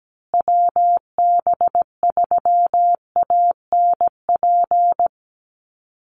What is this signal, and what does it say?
Morse code, heard as a keyed tone, about 17 wpm, 700 Hz. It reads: WB3ANP
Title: USA Call Signs Repeated 2x 17wpm